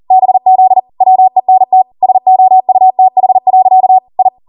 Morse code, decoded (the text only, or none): BZ WENT SOUTH. I